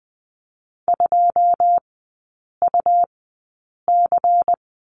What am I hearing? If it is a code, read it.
Morse code, heard as a keyed tone, about 20 wpm, spaced as for 8 wpm, 700 Hz. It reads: 2UC
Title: 114 Lesson 19e - Letter w - Three Characters 20wpm